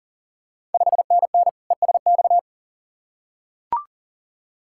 Morse code, heard as a keyed tone, about 40 wpm, 690 Hz. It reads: 5NN ESX
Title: Ontario QSO Party Exchange 40wpm